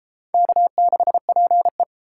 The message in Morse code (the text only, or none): K6PE